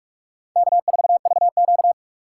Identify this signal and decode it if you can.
Morse code, heard as a keyed tone, about 40 wpm, 700 Hz. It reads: K4VQ